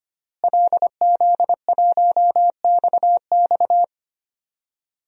Morse code, heard as a keyed tone, about 25 wpm, 700 Hz. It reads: LZ1XX
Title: DXCC-100 Call Signs - 5 Characters - Repeated 2x - 25wpm